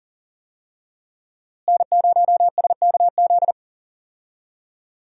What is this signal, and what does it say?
Morse code, heard as a keyed tone, about 40 wpm, 700 Hz. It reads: N0SKZ
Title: USA Call Signs - 5 Characters - Repeated 3x - 40wpm